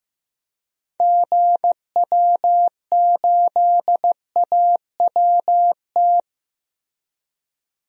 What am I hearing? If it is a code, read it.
Morse code, heard as a keyed tone, about 15 wpm, 700 Hz. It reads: GW8AWT
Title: DXCC-100 Call Signs - 6 Characters - 15wpm